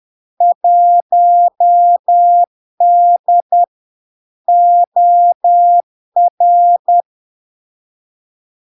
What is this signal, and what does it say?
Morse code, heard as a keyed tone, about 10 wpm, 700 Hz. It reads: 1D OR
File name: LARK-FD-2022-Log-in-CW-10wpm-700-Hz